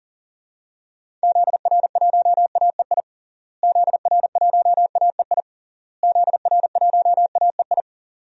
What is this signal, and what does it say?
Morse code, heard as a keyed tone, about 37 wpm, 700 Hz. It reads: ZR1AEI ZR1AEI ZR1AEI
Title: DXCC-100 Call Signs - 6 Characters - Repeated 3x - 40wpm